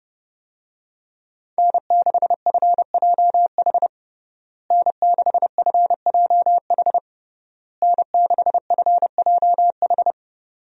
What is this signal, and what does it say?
Morse code, heard as a keyed tone, about 30 wpm, 700 Hz. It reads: N6FJH N6FJH N6FJH
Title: USA Call Signs - 5 Characters - Repeated 3x - 30wpm